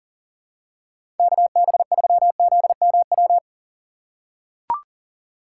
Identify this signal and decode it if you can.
Morse code, heard as a keyed tone, about 40 wpm, 700 Hz. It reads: KB3ZMW
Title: USA Call Signs - 6 Characters - 2x Character Spacing - 40wpm